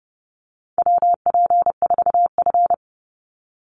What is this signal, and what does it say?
Morse code, heard as a keyed tone, about 30 wpm, 700 Hz. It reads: WP4F
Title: Call Signs Repeated 2x - 30wpm